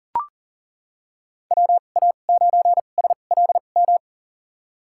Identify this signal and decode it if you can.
Morse code, heard as a keyed tone, about 42 wpm, spaced as for 30 wpm, 690 Hz. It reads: WA9SLM
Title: USA Call Signs - 6 Characters - 2x Character Spacing - 40wpm